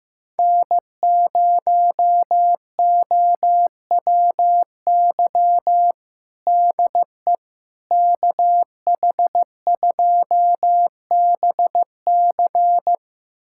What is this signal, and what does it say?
Morse code, heard as a keyed tone, about 15 wpm, 700 Hz. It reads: N0OWY DE KH2BC
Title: USA Callsign DE Callsign - 5 Characters - 15wpm